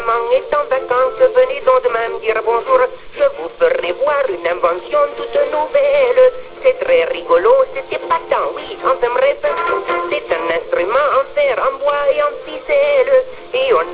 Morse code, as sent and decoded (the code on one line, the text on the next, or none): none